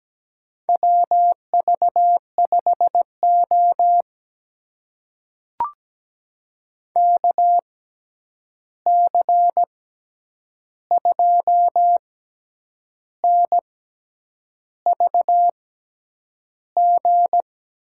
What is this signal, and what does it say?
Morse code, heard as a keyed tone, about 17 wpm, spaced as for 8 wpm, 700 Hz. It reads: WV5O KC2NVG
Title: USA Call Signs - 6x Character Spacing - 17wpm